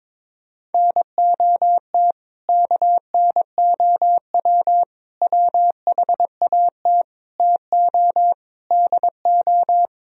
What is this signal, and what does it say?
Morse code, heard as a keyed tone, about 23 wpm, 700 Hz. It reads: NOT KNOW WHAT TO DO
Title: Speed-Racing - Sets of 5 Words 22wpm